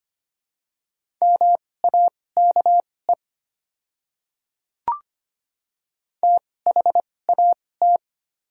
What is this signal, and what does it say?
Morse code, heard as a keyed tone, about 25 wpm, spaced as for 18 wpm, 700 Hz. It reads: MAKE THAT
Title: 2x Character Spacing - Top 100 Words - 25wpm